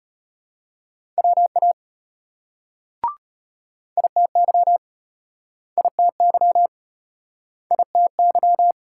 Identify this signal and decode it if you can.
Morse code, heard as a keyed tone, about 33 wpm, 700 Hz. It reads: WA ITY ITY ITY